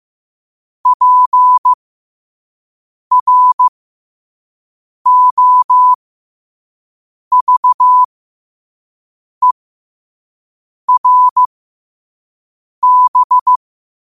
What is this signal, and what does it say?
Morse code, heard as a keyed tone, about 15 wpm, spaced as for 5 wpm, 1000 Hz. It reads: PROVERB
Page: New quotes every day in morse code at 5 Words per minute.